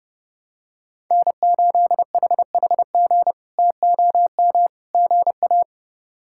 Morse code, heard as a keyed tone, about 30 wpm, 700 Hz. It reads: N8HHG TOM GA
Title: K1USN SST QSO Practice - 30wpm